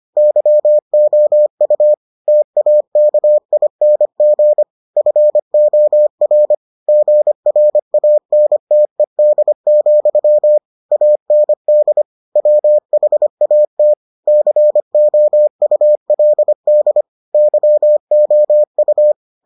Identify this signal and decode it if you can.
Morse code, heard as a keyed tone, about 25 wpm, 600 Hz. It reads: YOU TAKING FOR GRANTED, AND WHAT COULD YOU